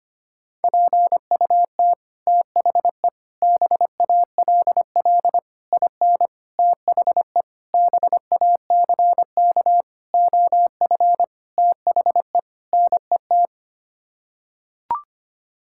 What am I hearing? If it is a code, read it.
Morse code, heard as a keyed tone, about 25 wpm, 700 Hz. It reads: PUT THE BALL IN THE BACK OF THE NET
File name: Sets of 9 Words - 25wpm